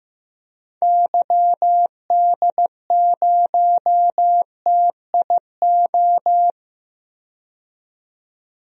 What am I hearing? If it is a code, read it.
Morse code, heard as a keyed tone, about 15 wpm, 700 Hz. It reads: YD0TIO
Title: DXCC-100 Call Signs - 6 Characters - 15wpm